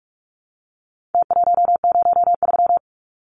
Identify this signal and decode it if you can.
Morse code, heard as a keyed tone, about 44 wpm, 700 Hz. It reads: T103
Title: Numbers - Contest Style - 4-Digit 45wpm